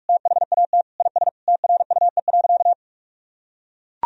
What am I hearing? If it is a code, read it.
Morse code, heard as a keyed tone, about 41 wpm, 700 Hz. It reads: THAT IS TRUE.
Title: Sentences from Top 1000 Words - Review Part 4 - 45wpm